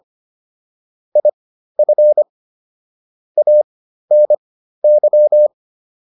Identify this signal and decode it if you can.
Morse code, heard as a keyed tone, about 25 wpm, spaced as for 13 wpm, 600 Hz. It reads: IF ANY